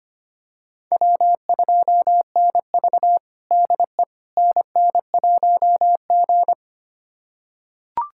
Morse code, heard as a keyed tone, about 25 wpm, 700 Hz. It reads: W2NV DE NN1G
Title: USA Callsign DE Callsign - 4 Characters - 25wpm